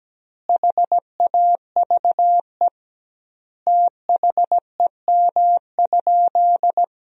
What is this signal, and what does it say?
Morse code, heard as a keyed tone, about 17 wpm, 700 Hz. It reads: HAVE THEM?
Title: 2x Word-Spacing - Sentences from Top 100 Words - 17wpm